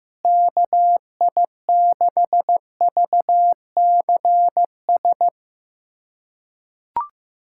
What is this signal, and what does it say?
Morse code, heard as a keyed tone, about 15 wpm, 700 Hz. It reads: KI6VCS
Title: USA Callsign DE Callsign - 6 Characters - 15wpm